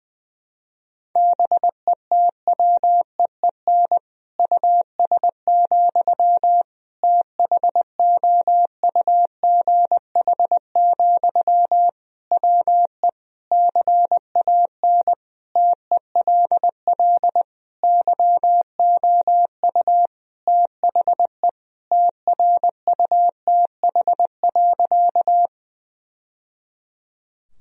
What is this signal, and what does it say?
Morse code, heard as a keyed tone, about 20 wpm, 700 Hz. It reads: BETWEEN US, THOUGH, WE CAN TELL YOU THE TRUTH.
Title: Sentences from Top 1000 Words - Review Part 2 - 20wpm